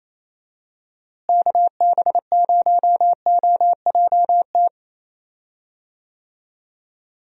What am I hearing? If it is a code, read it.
Morse code, heard as a keyed tone, about 28 wpm, 700 Hz. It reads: KB0OJT
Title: USA Call Signs - 6 Characters - Repeated 2x - 28wpm